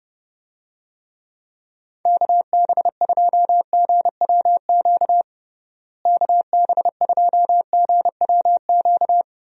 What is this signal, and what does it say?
Morse code, heard as a keyed tone, about 30 wpm, 700 Hz. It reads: KB2GWQ KB2GWQ
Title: USA Call Signs - 6 Characters - Repeated 3x - 30wpm